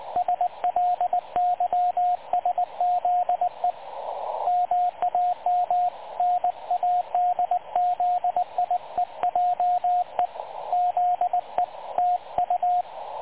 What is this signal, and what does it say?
Morse code, heard as a keyed tone, about 19 wpm, 720 Hz, with radio noise. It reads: SLYSZE MAM NADZIEJE ZE TU
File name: dlazaawansowanych / nagraniazpasma / websdr recording 3567.0kHz